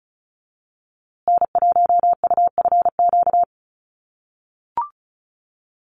Morse code, heard as a keyed tone, about 34 wpm, 700 Hz. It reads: N1UFQ
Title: USA Call Signs Repeated 3x 35wpm